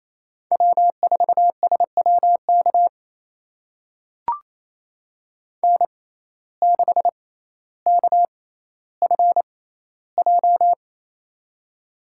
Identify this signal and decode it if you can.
Morse code, heard as a keyed tone, about 28 wpm, spaced as for 14 wpm, 700 Hz. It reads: W4SWK N6KFJ
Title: USA Call Signs - 5 Characters - 6x Character Spacing - 28wpm